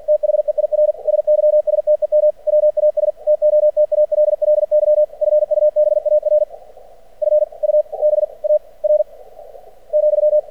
Morse code, heard as a keyed tone, about 50 wpm, 600 Hz. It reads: THEIR SYSTEM WAS TOTALLY FUBAR F U B A R ,
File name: archived / CRSnet